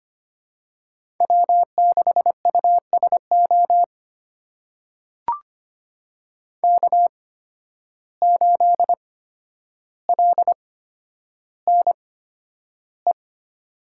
Morse code, heard as a keyed tone, about 25 wpm, spaced as for 11 wpm, 700 Hz. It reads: W6USO K8LNE
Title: USA Call Signs - 5 Characters - 8x Character Spacing - 25wpm